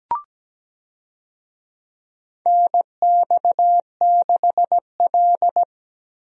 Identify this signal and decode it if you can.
Morse code, heard as a keyed tone, about 17 wpm, 700 Hz. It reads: NX6L
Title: USA Call Signs - 4 Characters - Repeated 2x - 17wpm